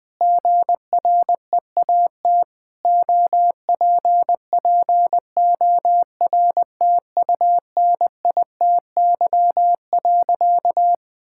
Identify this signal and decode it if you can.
Morse code, heard as a keyed tone, about 20 wpm, 700 Hz. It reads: GREAT OPPORTUNITY.
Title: Sentences with 4-Word Phrases - Spoken Beginning - 20wpm